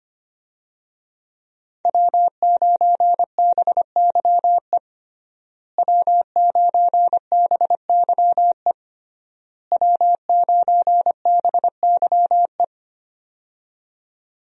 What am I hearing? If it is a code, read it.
Morse code, heard as a keyed tone, about 25 wpm, 700 Hz. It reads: W9BYE W9BYE W9BYE
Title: USA Call Signs - 5 Characters - Repeated 3x - 25wpm